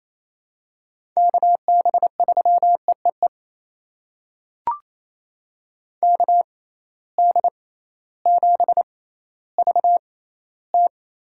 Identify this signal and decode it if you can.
Morse code, heard as a keyed tone, about 28 wpm, spaced as for 15 wpm, 700 Hz. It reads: KB3EEE KD7VT
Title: USA Call Signs - 6 Characters - 6x Character Spacing - 28wpm